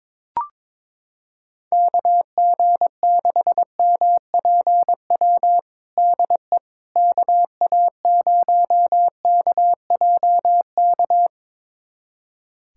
Morse code, heard as a keyed tone, about 21 wpm, 700 Hz. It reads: KG6MPW DE KA0KJK
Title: USA Callsign DE Callsign - 6 Characters - 22wpm